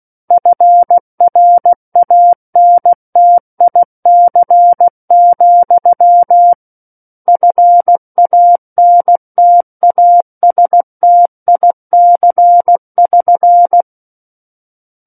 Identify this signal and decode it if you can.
Morse code, heard as a keyed tone, about 16 wpm, 700 Hz. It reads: FRANTIC, FANTASTIC<SN>
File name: Book One / 07-Chapter 7 - 16 WPM